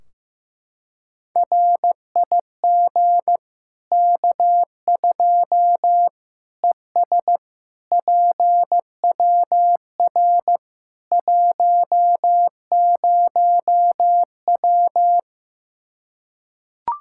Morse code, heard as a keyed tone, about 15 wpm, 700 Hz. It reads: RIG K2 ES PWR 10W
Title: Practice QSO - Exchange 2 - Info - 15wpm